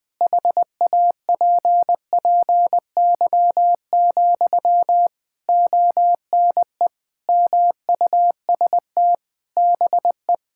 Morse code, heard as a keyed tone, about 20 wpm, 700 Hz. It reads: HAPPY, ONE MUST BE
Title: Sentences from Top 1000 Words - Review Part 6 - 20wpm